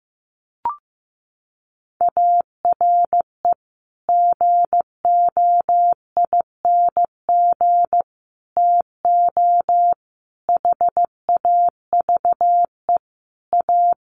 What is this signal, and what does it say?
Morse code, heard as a keyed tone, about 15 wpm, 700 Hz. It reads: ARE GOING TO HAVE A
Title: Sets of 5 Words Encore2 15wpm